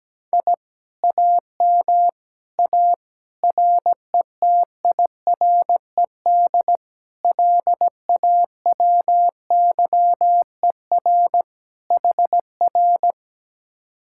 Morse code, heard as a keyed tone, about 17 wpm, 700 Hz. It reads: I AM A RETIRED LAWYER HR
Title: Speed-Racing Common QSO Phrases - 17wpm